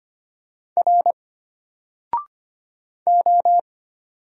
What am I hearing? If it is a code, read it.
Morse code, heard as a keyed tone, about 25 wpm, 700 Hz. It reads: R O
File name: Single-Letter-Number-25wpm